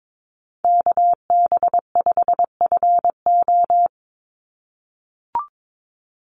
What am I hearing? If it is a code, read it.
Morse code, heard as a keyed tone, about 22 wpm, 700 Hz. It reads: KB5FO
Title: USA Call Signs 22wpm